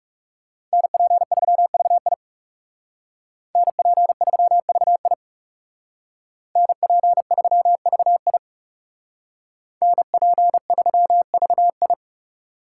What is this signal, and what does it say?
Morse code, heard as a keyed tone, about 38 wpm, 700 Hz. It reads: NP3VI NP3VI NP3VI NP3VI